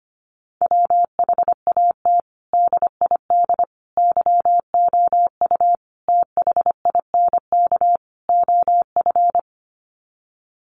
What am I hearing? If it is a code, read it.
Morse code, heard as a keyed tone, about 25 wpm, 700 Hz. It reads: WHAT DID YOU THINK OF
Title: Sets of 5 Words Encore2 25wpm